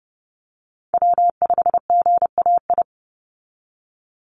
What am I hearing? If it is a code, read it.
Morse code, heard as a keyed tone, about 30 wpm, 700 Hz. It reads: W5GAI
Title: Call Signs Repeated 2x - 30wpm